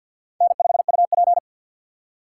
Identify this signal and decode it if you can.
Morse code, heard as a keyed tone, about 53 wpm, 700 Hz. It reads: N5UP
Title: USA Call Signs - 2x Character Spacing - 50wpm